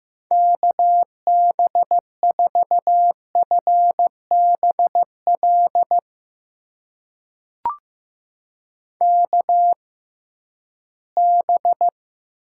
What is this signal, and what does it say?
Morse code, heard as a keyed tone, about 15 wpm, spaced as for 11 wpm, 700 Hz. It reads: KB4FBL KB
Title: USA Call Signs - 6 Characters - 6x Character Spacing - 15wpm